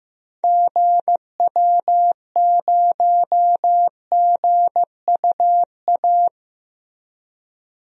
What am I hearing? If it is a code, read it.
Morse code, heard as a keyed tone, about 15 wpm, 700 Hz. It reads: GW0GUA
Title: DXCC-100 Call Signs - 6 Characters - Repeated 2x - 15wpm